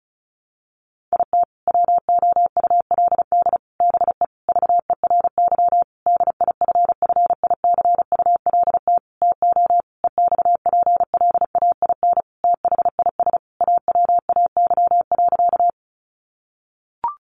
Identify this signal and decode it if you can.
Morse code, heard as a keyed tone, about 34 wpm, 700 Hz. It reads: IT WOULD BE VERY DIFFICULT TO EXPLAIN THIS AWAY.
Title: Sentences from Top 1100 Words 35wpm